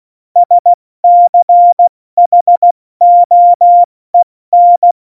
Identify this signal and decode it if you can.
Morse code, heard as a keyed tone, about 16 wpm, 700 Hz. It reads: SCHOEN
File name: Morsekurs / Witze / Witz 15. August 17, Tempo 70 F80